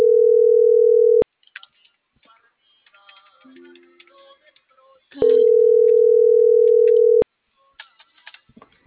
Ringback tone (North America), 440 + 480 Hz.